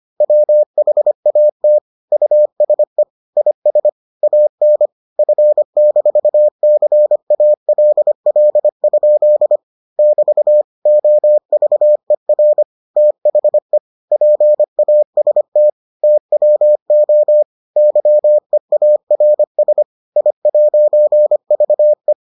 Morse code, heard as a keyed tone, about 25 wpm, 600 Hz. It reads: WHAT USE IS AN F-CALL? <BT> OVER THE PAST TWO YEARS I'VE